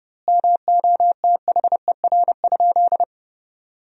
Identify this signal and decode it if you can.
Morse code, heard as a keyed tone, about 32 wpm, 700 Hz. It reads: MOTHER?
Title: 2x Word-Spacing - Sentences from Top 200 Words - 30wpm